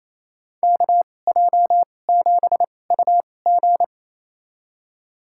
Morse code, heard as a keyed tone, about 28 wpm, spaced as for 20 wpm, 700 Hz. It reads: KJ7UG
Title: USA Call Signs - 2x Character Spacing - 28wpm